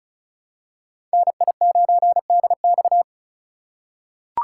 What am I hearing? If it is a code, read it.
Morse code, heard as a keyed tone, about 36 wpm, 700 Hz. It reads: NI9DX
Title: USA Call Signs - 5 Characters - 4x Character Spacing - 35wpm